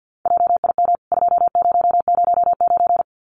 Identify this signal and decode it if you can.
Morse code, heard as a keyed tone, about 50 wpm, 700 Hz. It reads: JIM 2009
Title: CWT Exchange 50wpm